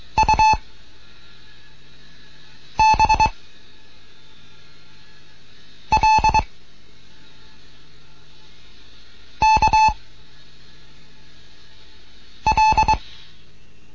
Morse code, heard as a keyed tone, about 22 wpm, 870 Hz. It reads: U B L K L